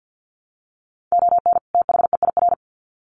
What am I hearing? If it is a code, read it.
Morse code, heard as a keyed tone, about 50 wpm, 700 Hz. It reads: ON THEIR
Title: Sets of 2 Words 50wpm